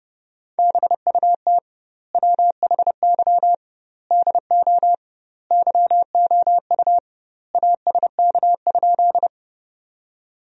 Morse code, heard as a keyed tone, about 30 wpm, 700 Hz. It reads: BUT WHY DO YOU ASK?
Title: 2x Word-Spacing - Sentences from Top 200 Words - 30wpm